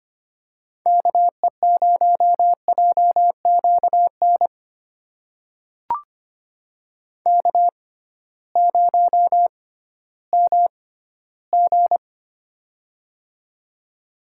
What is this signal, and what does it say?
Morse code, heard as a keyed tone, about 25 wpm, spaced as for 15 wpm, 700 Hz. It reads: KE0JQN K0MG
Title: USA Call Signs - 6x Character Spacing - 25wpm